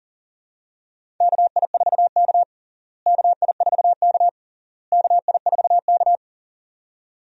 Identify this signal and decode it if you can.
Morse code, heard as a keyed tone, about 40 wpm, 700 Hz. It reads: KI4K KI4K KI4K
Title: USA Call Signs - 4 Characters - Repeated 3x - 40wpm